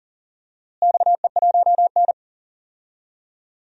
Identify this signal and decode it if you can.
Morse code, heard as a keyed tone, about 40 wpm, 700 Hz. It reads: XE1N
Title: DXCC-100 Call Signs - 4 Characters - Repeated 2x - 40wpm